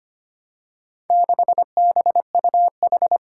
Morse code, heard as a keyed tone, about 25 wpm, 700 Hz. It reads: 6BUH
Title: Four Challenging Letter-Number Combinations 25wpm